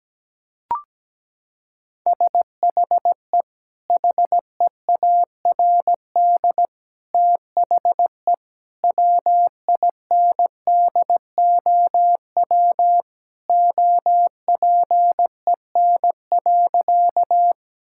Morse code, heard as a keyed tone, about 17 wpm, 700 Hz. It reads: SHE HEARD THE WINDOW OPEN.
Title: Sentences from Top 1600 Words 17wpm